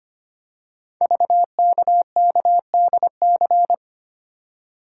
Morse code, heard as a keyed tone, about 25 wpm, 700 Hz. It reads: VKKDC
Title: Code Groups - 5 Letters - 25wpm